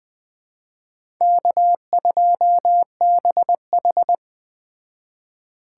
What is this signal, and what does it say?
Morse code, heard as a keyed tone, about 20 wpm, 700 Hz. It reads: K2BH